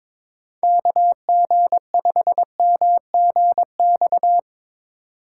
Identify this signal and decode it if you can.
Morse code, heard as a keyed tone, about 21 wpm, 700 Hz. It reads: KG5MGX
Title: USA Call Signs - 6 Characters - 6x Character Spacing - 22wpm